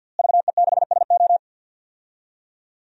Morse code, heard as a keyed tone, about 47 wpm, 700 Hz. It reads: VE6SO
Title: DXCC-100 Call Signs - 5 Characters - Repeated 3x - 50wpm